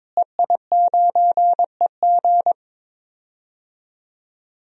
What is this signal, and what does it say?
Morse code, heard as a keyed tone, about 22 wpm, 700 Hz. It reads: EI9EG
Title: DXCC-100 Call Signs - 5 Characters - Repeated 3x - 22wpm